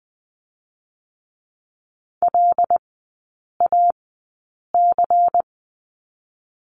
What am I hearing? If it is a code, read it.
Morse code, heard as a keyed tone, about 20 wpm, spaced as for 8 wpm, 700 Hz. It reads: LAC
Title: 107 Lesson 18e - Letter m - Three Characters 20wpm